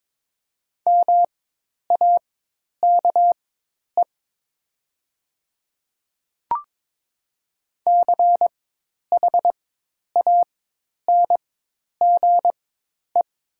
Morse code, heard as a keyed tone, about 22 wpm, spaced as for 10 wpm, 700 Hz. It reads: MAKE CHANGE
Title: 4x Character Spacing - Top 200 Words - 22wpm